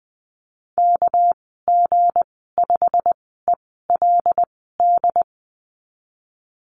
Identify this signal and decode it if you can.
Morse code, heard as a keyed tone, about 20 wpm, spaced as for 14 wpm, 700 Hz. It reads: KG5ELD
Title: 219 Lesson 35g - Number 0 - Call Signs 20wpm